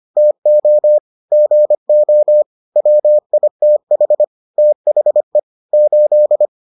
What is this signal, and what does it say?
Morse code, heard as a keyed tone, about 25 wpm, 600 Hz. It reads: TO GO WITH THE 8